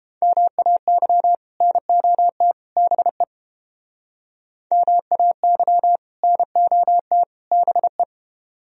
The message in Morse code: MAY NOT BE MAY NOT BE

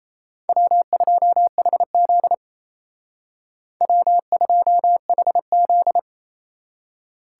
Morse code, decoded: W2HZ W2HZ

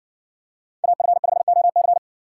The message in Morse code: AF5OC

